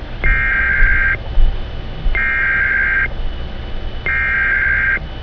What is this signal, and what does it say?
Emergency Alert System (SAME) tones (header bursts), heard as harsh buzzing bursts. It is heard from 0.3 to 5.0 s.
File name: scanner / eas